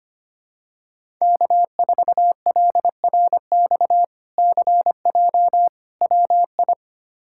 Morse code, heard as a keyed tone, about 25 wpm, 700 Hz. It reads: K4LRX CJ WI
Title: K1USN SST QSO Practice - 25wpm